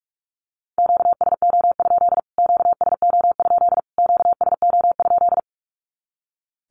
Morse code, heard as a keyed tone, about 44 wpm, 700 Hz. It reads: QSO? QSO? QSO?
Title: Q-Codes Repeated 3x 45wpm